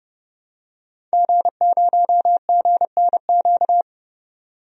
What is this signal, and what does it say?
Morse code, heard as a keyed tone, about 30 wpm, 700 Hz. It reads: G0GNQ
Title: DXCC-100 Call Signs - 5 Characters - 30wpm